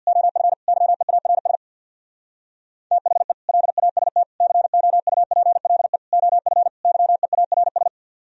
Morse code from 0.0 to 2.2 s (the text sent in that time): OF YEARS